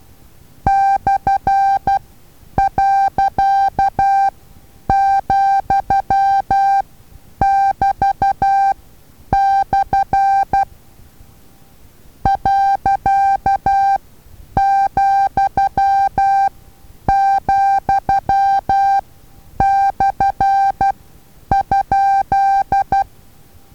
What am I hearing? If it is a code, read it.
Morse code, heard as a keyed tone, about 12 wpm, spaced as for 9 wpm, 790 Hz. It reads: /.,<BT>/ .,,/?
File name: Taviro HA6VA A